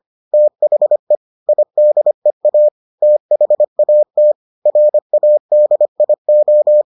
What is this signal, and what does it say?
Morse code, heard as a keyed tone, about 25 wpm, 600 Hz. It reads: THE IDEA THAT RADIO